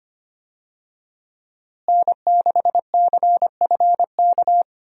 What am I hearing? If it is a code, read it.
Morse code, heard as a keyed tone, about 25 wpm, 700 Hz. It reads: N6CFK
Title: USA Call Signs - 5 Characters - Repeated 3x - 25wpm